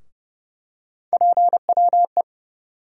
Morse code, heard as a keyed tone, about 30 wpm, 700 Hz. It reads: PWE
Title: USA Call Sign Postfixes - 3 Letters 30wpm